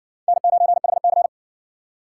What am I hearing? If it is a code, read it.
Morse code, heard as a keyed tone, about 55 wpm, 700 Hz. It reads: N9HC